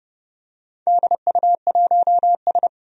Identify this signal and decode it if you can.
Morse code, heard as a keyed tone, about 29 wpm, 700 Hz. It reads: DU1S
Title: DXCC-100 Call Signs - 4 Characters - 30wpm